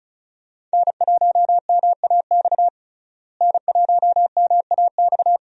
Morse code, heard as a keyed tone, about 36 wpm, 700 Hz. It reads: N1MAX N1MAX
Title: USA Call Signs - 5 Characters - Repeated 3x - 35wpm